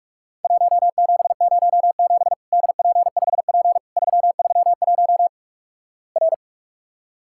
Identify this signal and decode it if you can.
Morse code, heard as a keyed tone, about 48 wpm, 700 Hz. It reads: 1707 DP5P 331 R
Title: Worked All Europe DX Contest - QTC Exchange - 45wpm